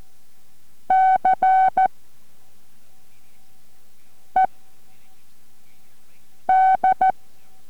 Morse code, heard as a keyed tone, about 14 wpm, spaced as for 4 wpm, 750 Hz. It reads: CED